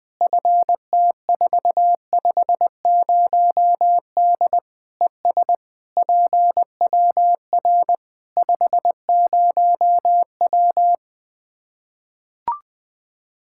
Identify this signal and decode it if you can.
Morse code, heard as a keyed tone, about 20 wpm, 700 Hz. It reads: FT450D ES PWR 50W
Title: K1USN SST QSO Practice 20wpm